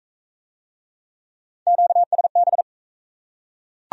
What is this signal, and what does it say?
Morse code, heard as a keyed tone, about 37 wpm, 700 Hz. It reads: QSB